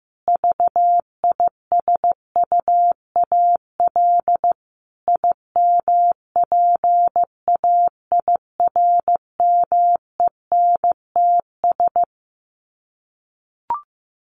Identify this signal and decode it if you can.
Morse code, heard as a keyed tone, about 15 wpm, 700 Hz. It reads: VISUAL IMPAIRMENTS
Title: Sets of 5 Words Encore 15wpm